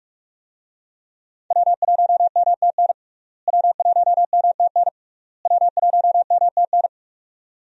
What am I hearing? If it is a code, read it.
Morse code, heard as a keyed tone, about 46 wpm, 700 Hz. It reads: W1MTN W1MTN W1MTN
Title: USA Call Signs - 5 Characters - Repeated 3x - 45wpm